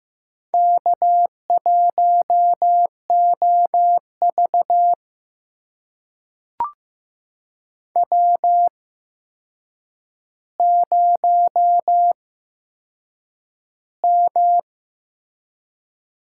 Morse code, heard as a keyed tone, about 15 wpm, spaced as for 7 wpm, 700 Hz. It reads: K1OV W0M
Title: USA Call Signs - 4 Characters - 8x Character Spacing - 15wpm